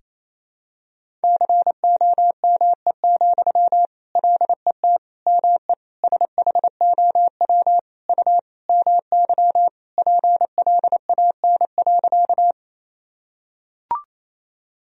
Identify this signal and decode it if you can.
Morse code, heard as a keyed tone, about 28 wpm, 700 Hz. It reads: COME, LET ME SHOW U MY PLAN.
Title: Sentences from Top 500 Words and Abbreviations 28wpm